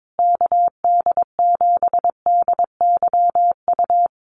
Morse code, heard as a keyed tone, about 22 wpm, 700 Hz. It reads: KD7DYU
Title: USA Call Signs Repeated 3x 22wpm